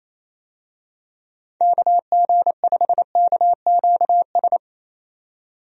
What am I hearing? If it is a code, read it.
Morse code, heard as a keyed tone, about 27 wpm, 700 Hz. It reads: KG5KQS
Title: USA Call Signs - 6 Characters - 28wpm